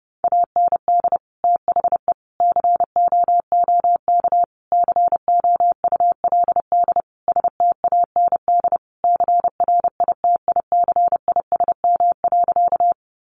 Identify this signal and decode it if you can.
Morse code, heard as a keyed tone, about 30 wpm, 700 Hz. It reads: AND THE COOK COULD STAND CRITICISM.
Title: Sentences from Top 1300 Words 30wpm